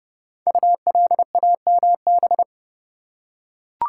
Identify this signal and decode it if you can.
Morse code, heard as a keyed tone, about 30 wpm, 700 Hz. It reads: ULAMB
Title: Code Groups - 5 Letters - 30wpm